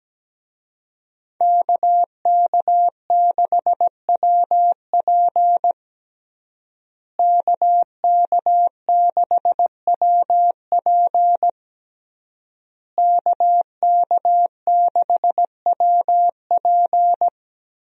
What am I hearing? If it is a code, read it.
Morse code, heard as a keyed tone, about 17 wpm, 700 Hz. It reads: KK6WP KK6WP KK6WP
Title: USA Call Signs - 5 Characters - Repeated 3x - 17wpm